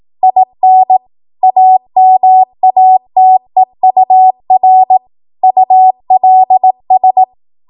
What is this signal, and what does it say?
Morse code, heard as a keyed tone, about 18 wpm, 750 Hz. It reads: IN AMATEUR ULS
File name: Morse / 18WPM